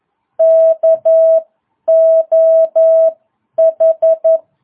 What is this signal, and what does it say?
Morse code, heard as a keyed tone, about 11 wpm, spaced as for 9 wpm, 660 Hz. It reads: KOH